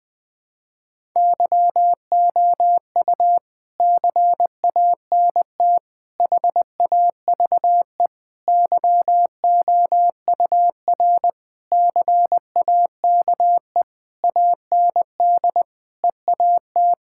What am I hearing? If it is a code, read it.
Morse code, heard as a keyed tone, about 20 wpm, 700 Hz. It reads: YOU CANT HAVE YOUR CAKE AND EAT
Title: Sentences with 9-Word Phrases - 20wpm